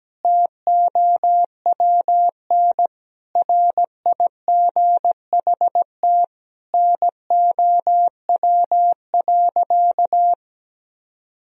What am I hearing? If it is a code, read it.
Morse code, heard as a keyed tone, about 17 wpm, 700 Hz. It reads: TOWN RIGHT NOW.
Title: Sentences from Top 1000 Words - Review Part 4 - 17wpm